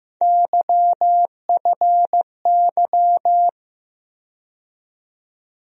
Morse code, heard as a keyed tone, about 15 wpm, 700 Hz. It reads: YFY
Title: USA Call Sign Postfixes - 3 Letters 15wpm